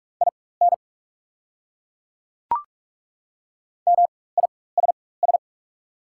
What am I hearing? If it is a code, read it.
Morse code, heard as a keyed tone, about 42 wpm, spaced as for 20 wpm, 690 Hz. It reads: IN MISS
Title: 4x Character Spacing - Top 500 Words - 45wpm